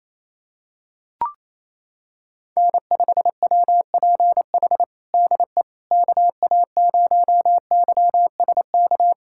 Morse code, heard as a keyed tone, about 28 wpm, 700 Hz. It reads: N5WPH DE KA0YSK
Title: USA Callsign DE Callsign 28wpm